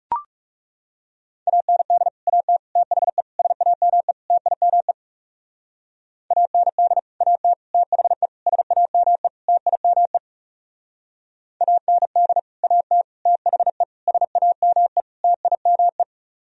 Morse code, heard as a keyed tone, about 37 wpm, 700 Hz. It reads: AND AT THE SAME TIME AND AT THE SAME TIME AND AT THE SAME TIME
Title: Speed-Racing - Sets of 5 Words 30wpm